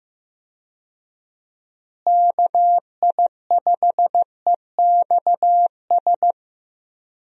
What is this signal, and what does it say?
Morse code, heard as a keyed tone, about 15 wpm, 700 Hz. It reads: KI5EXS
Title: USA Call Signs - 6 Characters - Repeated 3x - 15wpm